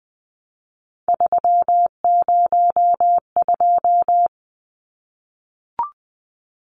Morse code, heard as a keyed tone, about 20 wpm, 700 Hz. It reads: 302
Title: Numbers - 3-Digit 20wpm